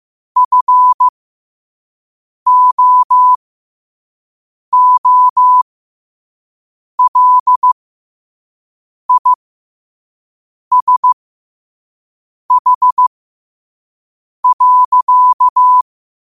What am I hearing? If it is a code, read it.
Morse code, heard as a keyed tone, about 15 wpm, spaced as for 5 wpm, 1000 Hz. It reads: FOOLISH.
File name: qotdfiles / 5wpm